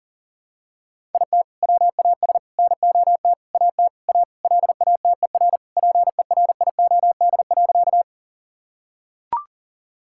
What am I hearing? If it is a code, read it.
Morse code, heard as a keyed tone, about 37 wpm, 700 Hz. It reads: IT WAS NOT AT A LATER PERIOD.
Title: Sentences from Top 1000 Words - Review Part 5 - 40wpm